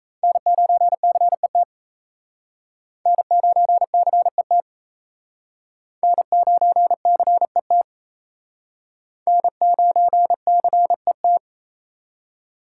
Morse code, heard as a keyed tone, about 36 wpm, 700 Hz. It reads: N9CET N9CET N9CET N9CET